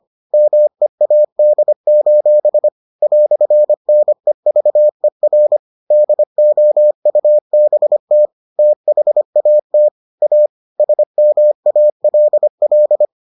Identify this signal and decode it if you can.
Morse code, heard as a keyed tone, about 25 wpm, 600 Hz. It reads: MEAD: "NEVER DOUBT THAT A SMALL